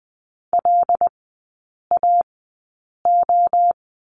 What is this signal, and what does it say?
Morse code, heard as a keyed tone, about 20 wpm, spaced as for 8 wpm, 700 Hz. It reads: LAO
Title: 107 Lesson 18e - Letter m - Three Characters 20wpm